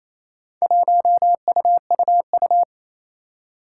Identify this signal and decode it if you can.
Morse code, heard as a keyed tone, about 27 wpm, 700 Hz. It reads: 1UUU
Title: Four Challenging Letter-Number Combinations 28wpm